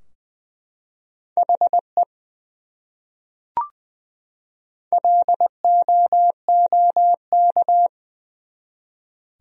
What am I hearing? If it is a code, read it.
Morse code, heard as a keyed tone, about 20 wpm, 700 Hz. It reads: HE LOOK
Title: Top 100 Words 20wpm